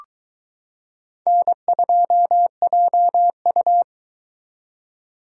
Morse code, heard as a keyed tone, about 22 wpm, 700 Hz. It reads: N2JU